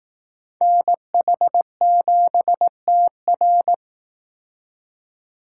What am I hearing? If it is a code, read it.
Morse code, heard as a keyed tone, about 19 wpm, 700 Hz. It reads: NH7TR
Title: Speed Racing - USA 2x2 Call Signs 15wpm